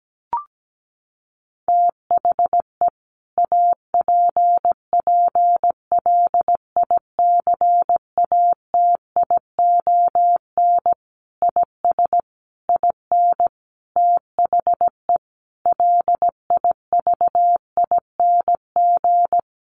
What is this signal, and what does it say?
Morse code, heard as a keyed tone, about 17 wpm, 700 Hz. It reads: THE APPLICATION IS IN THE LIVING
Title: Sentences from Top 1200 Words 17wpm